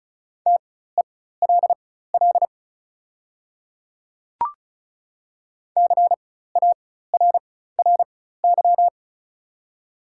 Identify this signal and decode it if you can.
Morse code, heard as a keyed tone, about 36 wpm, spaced as for 16 wpm, 690 Hz. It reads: TELL CARRY
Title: 4x Character Spacing - Top 300 Words - 35wpm